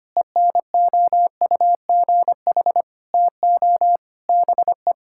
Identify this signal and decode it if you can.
Morse code, heard as a keyed tone, about 25 wpm, 700 Hz. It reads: ENOUGH TO BE
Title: Sentences from Top 1000 Words - Review Part 6 - 25wpm